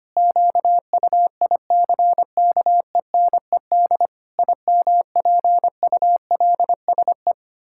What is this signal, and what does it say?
Morse code, heard as a keyed tone, about 25 wpm, 700 Hz. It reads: QUICKENED IMPULSE
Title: Pangram Challenge v2 - 25wpm